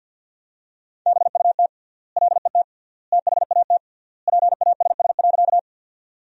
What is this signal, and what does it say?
Morse code, heard as a keyed tone, about 50 wpm, 700 Hz. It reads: BUT LET THAT PASS.
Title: 3x Word-Spacing - Sentences from Top 400 Words - 50wpm